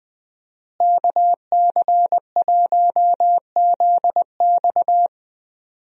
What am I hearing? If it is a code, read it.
Morse code, heard as a keyed tone, about 20 wpm, 700 Hz. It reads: KC1ZX
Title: USA Call Signs - 5 Characters - Repeated 2x - 20wpm